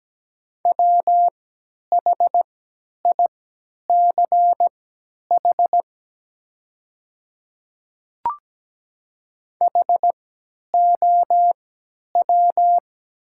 Morse code, heard as a keyed tone, about 17 wpm, spaced as for 10 wpm, 700 Hz. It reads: WHICH HOW
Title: 3x Character Spacing - Top 200 Words - 17wpm